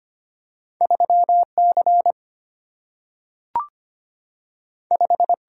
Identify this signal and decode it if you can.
Morse code, heard as a keyed tone, about 25 wpm, 690 Hz. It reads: 3 C 5
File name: Two Letter-Number Combinations - 25wpm